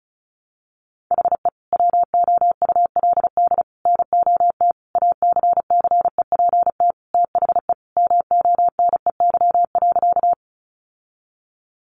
Morse code, heard as a keyed tone, about 34 wpm, 700 Hz. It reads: HE WOULD NOT ACCEPT THE MONEY.
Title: Sentences from Top 1300 Words 35wpm